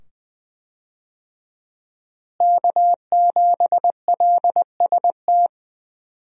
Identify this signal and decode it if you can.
Morse code, heard as a keyed tone, about 20 wpm, 700 Hz. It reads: K7LST
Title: USA Call Signs - 5 Characters - Repeated 2x - 20wpm